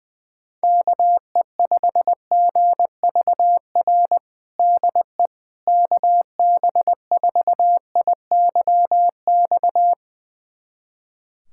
Morse code, heard as a keyed tone, about 20 wpm, 700 Hz. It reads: KE5GVR DE KB4IYX
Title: USA Callsign DE Callsign 20wpm